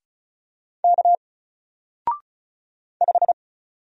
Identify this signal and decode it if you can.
Morse code, heard as a keyed tone, about 33 wpm, 700 Hz. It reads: K 5
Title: Single Letter-Number 35wpm